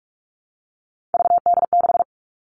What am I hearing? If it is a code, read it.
Morse code, heard as a keyed tone, about 44 wpm, 700 Hz. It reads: VD6
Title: All Cut Numbers - 3-Digit 45wpm